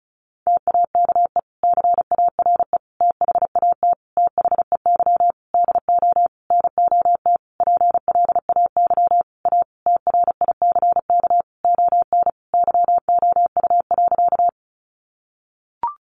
Morse code, heard as a keyed tone, about 34 wpm, 700 Hz. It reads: TAKE CARE THAT THEY DO NOT PLAY A TRICK ON YOU.
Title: Sentences from Top 1500 Words 35wpm